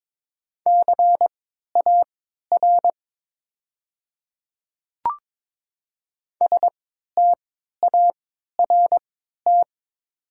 Morse code, heard as a keyed tone, about 22 wpm, spaced as for 12 wpm, 700 Hz. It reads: CAR START
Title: 3x Character Spacing - Top 500 Words - 22wpm